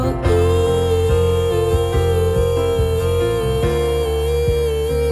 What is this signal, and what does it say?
Telephone call progress tone (China, 450 Hz), heard as a telephone network's tone, about 450 Hz, with noise. Busy tone.